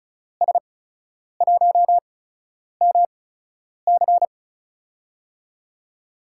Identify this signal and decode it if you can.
Morse code, heard as a keyed tone, about 36 wpm, spaced as for 10 wpm, 700 Hz. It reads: S1MC
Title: USA Call Signs - 8x Character Spacing - 35wpm